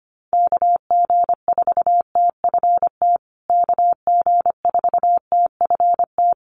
Morse code, heard as a keyed tone, about 25 wpm, 700 Hz. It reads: KG4TFT KG4TFT
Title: USA Call Signs Repeated 2x 25wpm